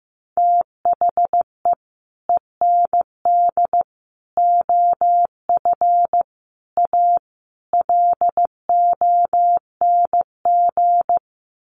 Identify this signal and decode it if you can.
Morse code, heard as a keyed tone, about 15 wpm, 700 Hz. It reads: THE END OF A LONG
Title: Sets of 5 Words Encore2 15wpm